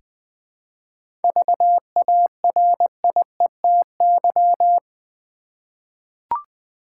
Morse code, heard as a keyed tone, about 20 wpm, 700 Hz. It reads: VARIETY
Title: 7 Letter Words - First 3 Letters Given - 20wpm